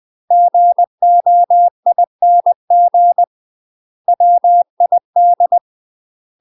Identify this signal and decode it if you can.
Morse code, heard as a keyed tone, about 20 wpm, 700 Hz. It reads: GOING WID
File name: UK MASTER QSOs TRAINING / 20WPM / UK MASTER QSOs 0053